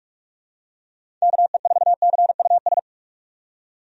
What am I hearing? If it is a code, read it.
Morse code, heard as a keyed tone, about 46 wpm, 700 Hz. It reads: KE4CUS
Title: USA Call Signs - 6 Characters - 45wpm